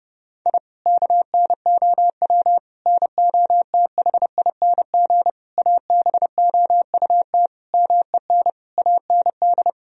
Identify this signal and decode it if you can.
Morse code, heard as a keyed tone, about 30 wpm, 700 Hz. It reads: I KNOW NOTHING ABOUT MEN AND
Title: Sentences from Top 1000 Words - Review Part 2 - 30wpm